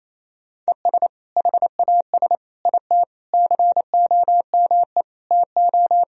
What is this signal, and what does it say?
Morse code, heard as a keyed tone, about 28 wpm, 700 Hz. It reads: ES HAS IT COME TO
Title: Sentences from Top 100 Words and Abbreviations 28wpm